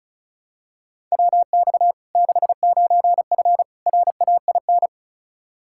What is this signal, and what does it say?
Morse code, heard as a keyed tone, about 36 wpm, 700 Hz. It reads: WX 69F RAIN
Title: Practice QSO - Exchange 2 - Info - 35wpm